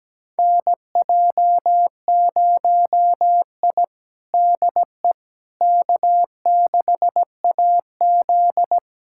Morse code, heard as a keyed tone, about 17 wpm, 700 Hz. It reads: NJ0I DE K6AZ
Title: USA Callsign DE Callsign - 4 Characters - 17wpm